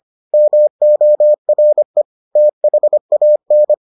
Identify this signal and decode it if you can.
Morse code, heard as a keyed tone, about 25 wpm, 600 Hz. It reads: MORE THAN